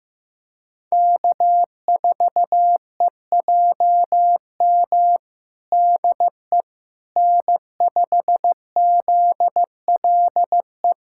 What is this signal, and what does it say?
Morse code, heard as a keyed tone, about 16 wpm, 700 Hz. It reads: K4EJM DE N5ZLE
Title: USA Callsign DE Callsign - 5 Characters - 15wpm